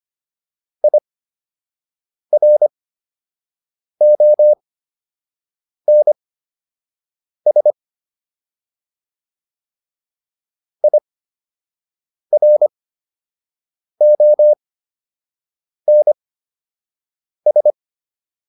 Morse code, heard as a keyed tone, about 25 wpm, spaced as for 6 wpm, 600 Hz. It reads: IRONS IRONS